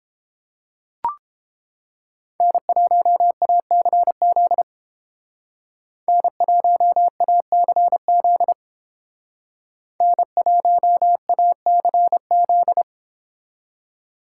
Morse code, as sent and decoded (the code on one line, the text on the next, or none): -. .---- .- -.-. --.. / -. .---- .- -.-. --.. / -. .---- .- -.-. --..
N1ACZ N1ACZ N1ACZ